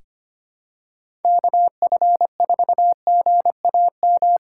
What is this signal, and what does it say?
Morse code, heard as a keyed tone, about 25 wpm, 700 Hz. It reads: KF4GAM
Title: USA Call Signs - 6 Characters - 2x Character Spacing - 25wpm